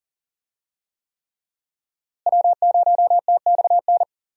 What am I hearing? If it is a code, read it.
Morse code, heard as a keyed tone, about 40 wpm, 700 Hz. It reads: W0TXN
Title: USA Call Signs - 5 Characters - Repeated 3x - 40wpm